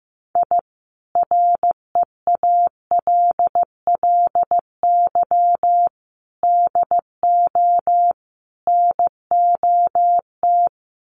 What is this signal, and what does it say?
Morse code, heard as a keyed tone, about 15 wpm, 700 Hz. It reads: I REALLY DO NOT
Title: Sets of 4 Words 15wpm